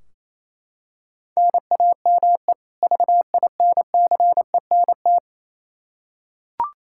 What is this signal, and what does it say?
Morse code, heard as a keyed tone, about 27 wpm, 700 Hz. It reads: NAME VINCENT
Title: Practice QSO - Exchange 1 - Beginning - 28wpm